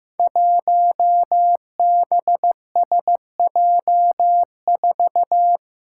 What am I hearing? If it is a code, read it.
Morse code, heard as a keyed tone, about 15 wpm, 700 Hz. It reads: 1BSJ4
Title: Five Challenging Letter-Number Combinations 15wpm